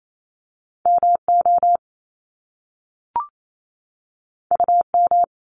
Morse code, heard as a keyed tone, about 28 wpm, 700 Hz. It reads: M O U M
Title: Top Two Letter Combinations in Words 28wpm